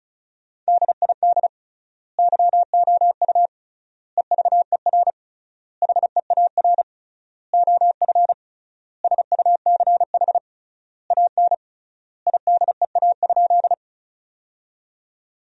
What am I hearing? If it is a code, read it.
Morse code, heard as a keyed tone, about 36 wpm, 700 Hz. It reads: DID YOU EVER HEAR OF SUCH AN IDEA?
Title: 3x Word-Spacing - Sentences from Top 400 Words - 35wpm